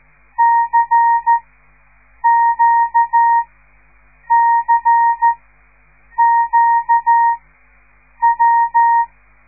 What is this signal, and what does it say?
Morse code, heard as a keyed tone, about 13 wpm, spaced as for 7 wpm, 930 Hz. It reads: CQCQW
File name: radiomorningstar / morse / les 5